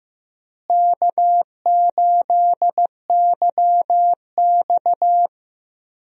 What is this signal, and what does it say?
Morse code, heard as a keyed tone, about 15 wpm, 700 Hz. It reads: K8YX
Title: USA Call Signs - 4 Characters - Repeated 2x - 15wpm